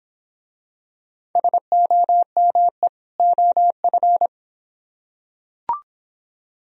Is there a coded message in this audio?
Yes: Morse code from 1.4 to 4.3 s, heard as a keyed tone, about 25 wpm, 700 Hz. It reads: SOME OF